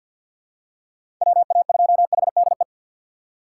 Morse code, heard as a keyed tone, about 47 wpm, 700 Hz. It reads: WA2HDE
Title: USA Call Signs - 6 Characters - 3x Character Spacing - 50wpm